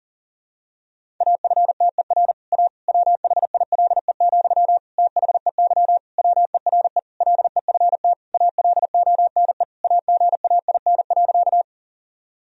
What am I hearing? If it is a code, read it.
Morse code, heard as a keyed tone, about 40 wpm, 700 Hz. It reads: AFTER A WHILE, THEY WERE LEFT ALONE AGAIN.
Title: Sentences from Top 1000 Words - Review Part 1 - 40wpm